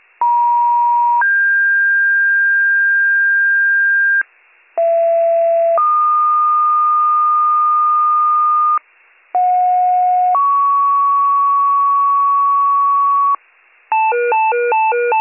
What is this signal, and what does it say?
Two-tone page, heard as a one-second tone, then a three-second tone, about 950 then 1600 Hz, about 670 then 1220 Hz, about 730 then 1120 Hz. It is heard from 0.2 to 13.4 s.